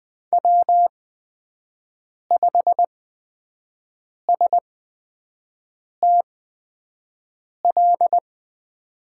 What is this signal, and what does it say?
Morse code, heard as a keyed tone, about 20 wpm, spaced as for 5 wpm, 700 Hz. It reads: W5STL
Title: USA Call Signs - 5 Characters - 8x Character Spacing - 20wpm